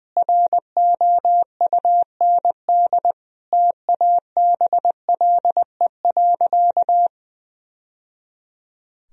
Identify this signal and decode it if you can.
Morse code, heard as a keyed tone, about 20 wpm, 700 Hz. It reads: ROUND TABLE.
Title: Sentences from Top 1000 Words - Review Part 2 - 20wpm